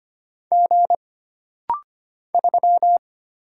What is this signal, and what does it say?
Morse code, heard as a keyed tone, about 25 wpm, 700 Hz. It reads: G 3
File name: Single Character (Letter:Number:Punctuation:Prosign) - Rapid-Fire - 25wpm